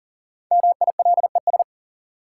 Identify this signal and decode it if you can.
Morse code, heard as a keyed tone, about 40 wpm, 700 Hz. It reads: MILES
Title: Sentences from Top 1000 Words - Review Part 3 - New Word Drill - 40wpm